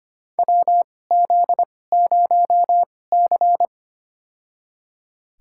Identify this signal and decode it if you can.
Morse code, heard as a keyed tone, about 25 wpm, spaced as for 18 wpm, 700 Hz. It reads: WZ0C
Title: USA Call Signs - 4 Characters - 2x Character Spacing - 25wpm